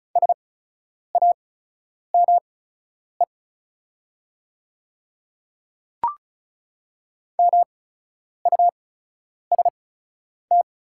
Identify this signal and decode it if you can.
Morse code, heard as a keyed tone, about 35 wpm, spaced as for 10 wpm, 700 Hz. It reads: SAME MUST
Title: 8x Character Spacing - Top 200 Words - 35wpm